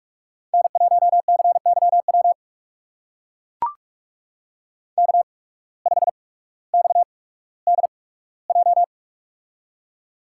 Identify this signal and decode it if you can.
Morse code, heard as a keyed tone, about 44 wpm, spaced as for 19 wpm, 690 Hz. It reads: N1CYW K5XDJ
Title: USA Call Signs - 5 Characters - 8x Character Spacing - 45wpm